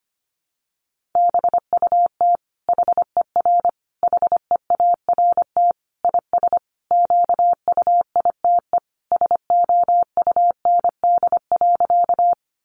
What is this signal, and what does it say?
Morse code, heard as a keyed tone, about 25 wpm, 700 Hz. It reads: BUT HER HEART IS QUITE SOUND.
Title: Sentences from Top 1200 Words 25wpm